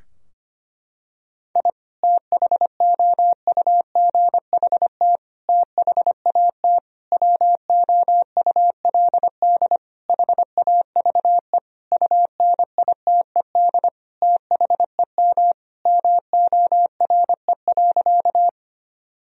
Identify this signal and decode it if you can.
Morse code, heard as a keyed tone, about 25 wpm, 700 Hz. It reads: I THOUGHT THAT WOULD HAVE UNITED THEM MORE.
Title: Sentences from Top 1000 Words - Review Part 4 - 25wpm